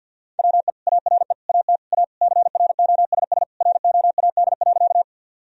Morse code, heard as a keyed tone, about 51 wpm, 700 Hz. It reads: WE ARE AT A CROSS ROAD.
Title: Sentences from Top 1000 Words - Review Part 2 - 50wpm